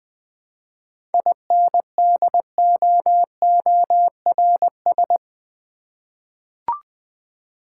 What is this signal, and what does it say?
Morse code, heard as a keyed tone, about 20 wpm, 690 Hz. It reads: INDOORS
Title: 7 Letter Words - First 3 Letters Given - 20wpm